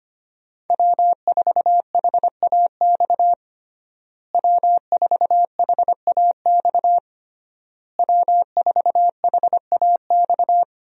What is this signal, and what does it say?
Morse code, heard as a keyed tone, about 25 wpm, 700 Hz. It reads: W4HAX W4HAX W4HAX
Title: USA Call Signs - 5 Characters - Repeated 3x - 25wpm